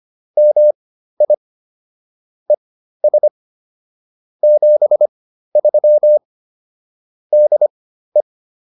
Morse code, heard as a keyed tone, about 25 wpm, spaced as for 13 wpm, 600 Hz. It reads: MI ES 73 DE
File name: POTA110 13